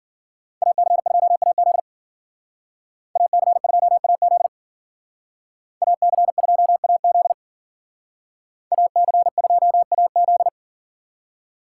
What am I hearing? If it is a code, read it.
Morse code, heard as a keyed tone, about 52 wpm, 700 Hz. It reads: AC2AZ AC2AZ AC2AZ AC2AZ